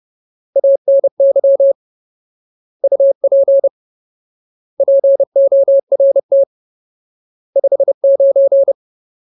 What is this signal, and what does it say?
Morse code, heard as a keyed tone, about 30 wpm, 550 Hz. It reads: ANY UP PORT 59
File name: Ultimate Word List / 30WPM Ultimate